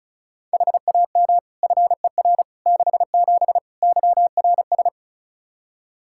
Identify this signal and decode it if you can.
Morse code, heard as a keyed tone, about 36 wpm, 700 Hz. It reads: HAM FER 67 YRS
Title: Practice QSO - Exchange 3 - More Info - 35wpm